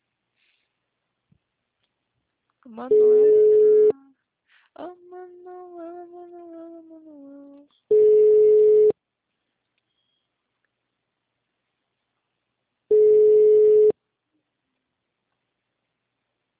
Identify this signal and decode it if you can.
Ringback tone (Europe, 425 Hz), with speech.
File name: out- -2025- - - .15967